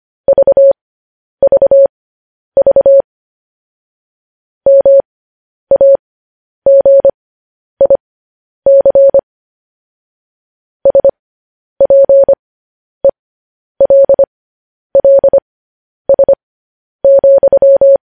Morse code, heard as a keyed tone, about 25 wpm, spaced as for 10 wpm, 550 Hz. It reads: VVV MAGIC SPELLS,
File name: podcast / News headlines-1x-25x10